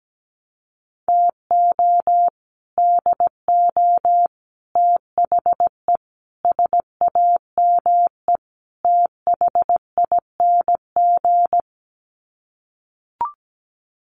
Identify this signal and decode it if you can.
Morse code, heard as a keyed tone, about 17 wpm, 700 Hz. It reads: TO DO THE SAME THING
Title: Sets of 5 Words Encore 17wpm